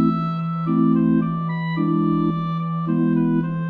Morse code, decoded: none